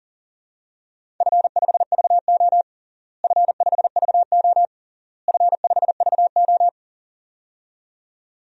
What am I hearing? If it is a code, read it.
Morse code, heard as a keyed tone, about 40 wpm, 700 Hz. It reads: F5VO F5VO F5VO
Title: DXCC-100 Call Signs - 4 Characters - Repeated 3x - 40wpm